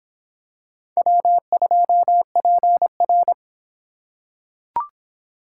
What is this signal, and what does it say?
Morse code, heard as a keyed tone, about 26 wpm, 700 Hz. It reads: W2PR